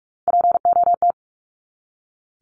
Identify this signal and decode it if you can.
Morse code, heard as a keyed tone, about 44 wpm, 700 Hz. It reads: POT
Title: Top 751-1000 Words 45wpm